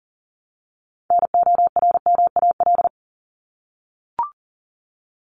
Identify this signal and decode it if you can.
Morse code, heard as a keyed tone, about 40 wpm, 700 Hz. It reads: NORMAL
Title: Top 1001-1250 Words 40wpm